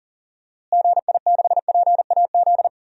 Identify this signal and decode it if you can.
Morse code, heard as a keyed tone, about 37 wpm, 700 Hz. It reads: GI6PAZ
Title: DXCC-100 Call Signs - 6 Characters - Repeated 3x - 40wpm